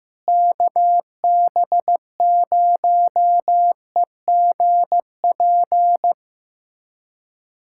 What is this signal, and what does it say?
Morse code, heard as a keyed tone, about 15 wpm, 700 Hz. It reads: KB0EGP
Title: USA Callsign DE Callsign 15wpm